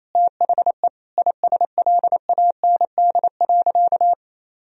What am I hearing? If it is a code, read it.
Morse code, heard as a keyed tone, about 28 wpm, 700 Hz. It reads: THE ISLAND.
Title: Sentences from Top 1000 Words - Review Part 3 - 28wpm